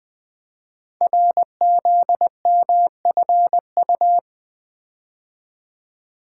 Morse code, heard as a keyed tone, about 20 wpm, 700 Hz. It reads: RZMFU
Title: Code Groups - 5 Letter:Numbers - 20wpm